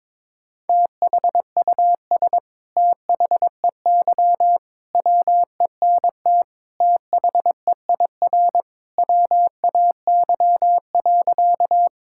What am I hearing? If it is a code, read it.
Morse code, heard as a keyed tone, about 22 wpm, 700 Hz. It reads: THUS THEY WENT THEIR WAY.
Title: Sentences from Top 1000 Words - Review Part 5 - 22wpm